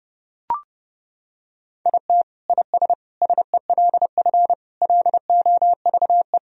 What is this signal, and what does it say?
Morse code, heard as a keyed tone, about 32 wpm, 700 Hz. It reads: IT IS SELF LOVE
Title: Sentences from Top 1000 Words - Review Part 1 - 30wpm